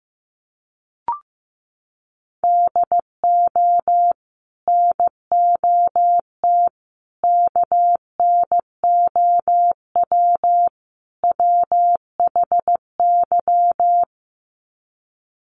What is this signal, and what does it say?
Morse code, heard as a keyed tone, about 15 wpm, 700 Hz. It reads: DO NOT KNOW WHY
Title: Sets of 4 Words 15wpm